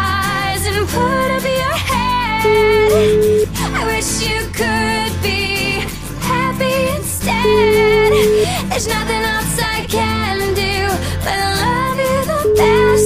Ringback tone (Europe, 425 Hz), with noise.